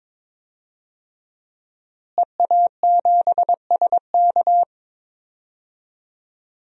Morse code, heard as a keyed tone, about 22 wpm, 700 Hz. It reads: EA7SK
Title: DXCC-100 Call Signs - 5 Characters - Repeated 3x - 22wpm